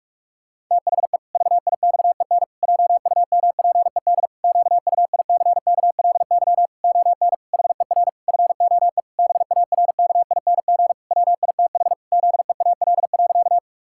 Morse code, heard as a keyed tone, about 46 wpm, 700 Hz. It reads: THE VIXEN JUMPED QUICKLY ON HER FOE BARKING WITH ZEAL.
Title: Pangram Challenge v2 - 45wpm